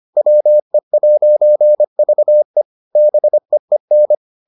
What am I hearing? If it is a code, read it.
Morse code, heard as a keyed tone, about 25 wpm, 600 Hz. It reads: WE'VE BEEN